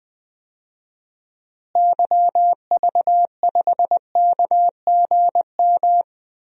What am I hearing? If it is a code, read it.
Morse code, heard as a keyed tone, about 20 wpm, 700 Hz. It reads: YV5KGM
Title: DXCC-100 Call Signs - 6 Characters - Repeated 3x - 20wpm